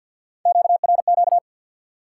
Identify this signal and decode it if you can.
Morse code, heard as a keyed tone, about 51 wpm, 700 Hz. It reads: QRQ
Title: Top 90 QSO Elements 50wpm